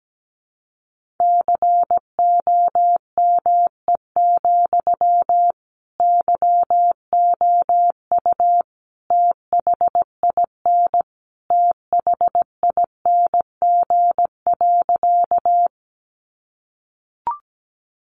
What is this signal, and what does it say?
Morse code, heard as a keyed tone, about 17 wpm, 700 Hz. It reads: COME, YOU THIN THING.
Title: Sentences from Top 1200 Words 17wpm